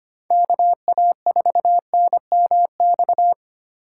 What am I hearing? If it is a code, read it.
Morse code, heard as a keyed tone, about 31 wpm, 700 Hz. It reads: KA4NMX
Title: Speed Racing - USA Call Signs 25wpm